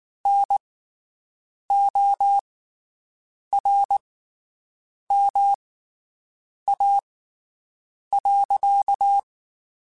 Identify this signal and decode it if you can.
Morse code, heard as a keyed tone, about 19 wpm, spaced as for 7 wpm, 800 Hz. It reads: NORMA.
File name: QSOs Spaced